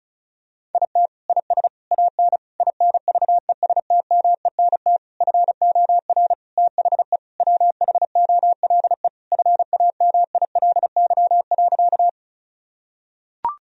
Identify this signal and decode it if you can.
Morse code, heard as a keyed tone, about 36 wpm, 700 Hz. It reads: IT IS AN INVESTMENT FOR THE WHOLE FAMILY.
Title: Sentences from Top 1000 Words - Review Part 4 - 35wpm